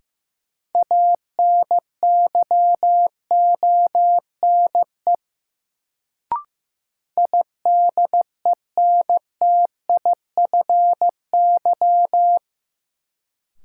Morse code, heard as a keyed tone, about 15 wpm, 700 Hz. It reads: ANYONE IDENTIFY
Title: Sentences from Top 900 Words - New Word Drill - 15wpm